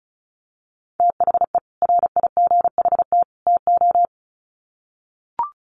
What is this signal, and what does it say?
Morse code, heard as a keyed tone, about 34 wpm, 700 Hz. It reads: THE RIGHT TO
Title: Sets of 3 Words 35wpm